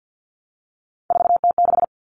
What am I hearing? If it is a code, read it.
Morse code, heard as a keyed tone, about 50 wpm, 700 Hz. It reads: 4T6
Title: All Cut Numbers - 3-Digit 50wpm